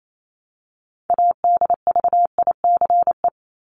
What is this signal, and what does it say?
Morse code, heard as a keyed tone, about 28 wpm, 700 Hz. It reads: ADVICE
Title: Top 1001-1250 Words 28wpm